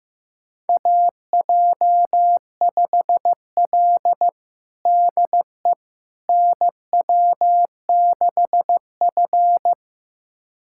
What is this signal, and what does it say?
Morse code, heard as a keyed tone, about 15 wpm, 700 Hz. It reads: AJ5L DE NW6F
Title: USA Callsign DE Callsign - 4 Characters - 15wpm